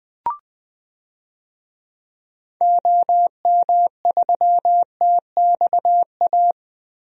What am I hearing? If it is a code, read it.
Morse code, heard as a keyed tone, about 20 wpm, 700 Hz. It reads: OM3TXA
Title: DXCC-100 Call Signs - 6 Characters - Repeated 2x - 20wpm